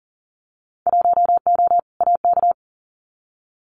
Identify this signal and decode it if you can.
Morse code, heard as a keyed tone, about 38 wpm, 700 Hz. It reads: 1O AK
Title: Winter Field Day Exchanges Encore 40wpm